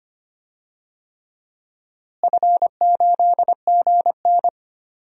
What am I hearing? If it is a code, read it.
Morse code, heard as a keyed tone, about 25 wpm, 700 Hz. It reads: F8GN
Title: DXCC-100 Call Signs - 4 Characters - Repeated 3x - 25wpm